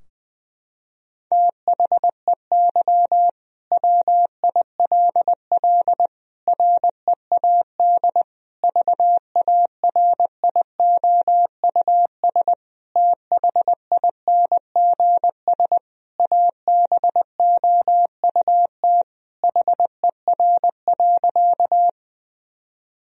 Morse code, heard as a keyed tone, about 20 wpm, 700 Hz. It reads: THEY WILL READ VARIOUS THINGS ABOUT HER.
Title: Sentences from Top 1000 Words - Review Part 4 - 20wpm